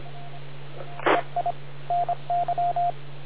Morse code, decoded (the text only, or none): EI NY